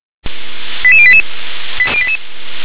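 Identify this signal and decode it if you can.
Five-tone selective call (ZVEI), heard as quick runs of five tones, tones of about 70 ms. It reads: 88088
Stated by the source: ZVEI five-tone selective call received on SDR.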